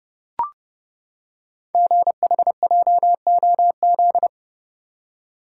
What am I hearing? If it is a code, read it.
Morse code, heard as a keyed tone, about 30 wpm, 690 Hz. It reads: GHJOZ
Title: Code Groups - 5 Letters - 30wpm